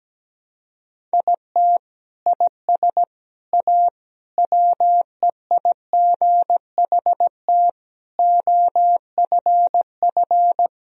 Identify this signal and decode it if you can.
Morse code, heard as a keyed tone, about 17 wpm, 700 Hz. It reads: IT IS A WEIGHT OFF
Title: Sentences from Top 1000 Words - Review Part 3 - 17wpm